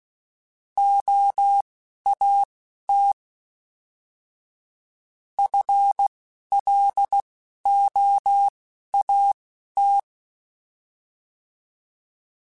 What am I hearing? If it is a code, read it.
Morse code, heard as a keyed tone, about 16 wpm, spaced as for 12 wpm, 800 Hz. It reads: OAT FLOAT
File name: FL / FL mp3s / Fast FL @10